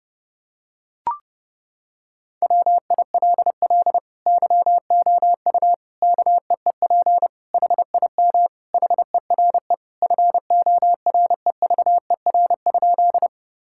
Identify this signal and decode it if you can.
Morse code, heard as a keyed tone, about 29 wpm, 700 Hz. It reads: WILL YOU KEEP HIM HERE FOREVER?
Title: Sentences from Top 1600 Words 30wpm